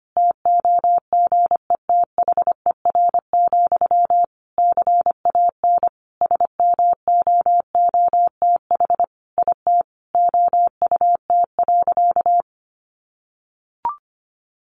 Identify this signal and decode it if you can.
Morse code, heard as a keyed tone, about 25 wpm, 700 Hz. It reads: TOGETHER, CAN SMOOTH IT OUT.
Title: Sentences from Top 1400 Words 25wpm